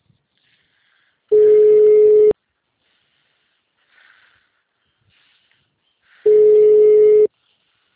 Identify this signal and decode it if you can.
Ringback tone (Europe, 425 Hz).
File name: out- -2007- - - .493